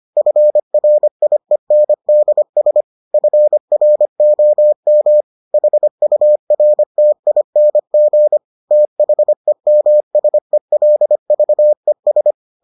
Morse code, heard as a keyed tone, about 25 wpm, 600 Hz. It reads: FRIENDS FROM HURTING THEMSELVES